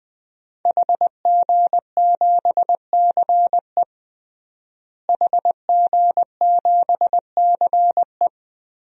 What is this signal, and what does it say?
Morse code, heard as a keyed tone, about 21 wpm, 700 Hz. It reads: HG7CE HG7CE
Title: DXCC-100 Call Signs - 5 Characters - Repeated 3x - 20wpm